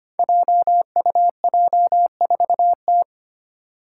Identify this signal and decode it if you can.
Morse code, heard as a keyed tone, about 25 wpm, 700 Hz. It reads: JUJ4T
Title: Five Challenging Letter-Number Combinations 25wpm